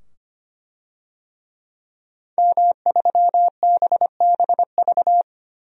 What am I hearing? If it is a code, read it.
Morse code, heard as a keyed tone, about 25 wpm, 700 Hz. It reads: M3BBV
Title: DXCC-100 Call Signs - 5 Characters - 25wpm